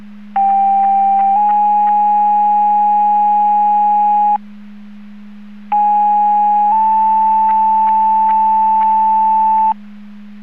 Two-tone page, heard as a one-second tone, then a three-second tone, about 790 then 830 Hz, about 830 then 880 Hz. It is heard from 0.4 to 9.7 s.